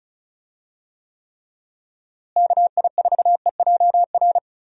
Morse code, heard as a keyed tone, about 34 wpm, 700 Hz. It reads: KI4EJR
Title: USA Call Signs - 6 Characters - Repeated 3x - 35wpm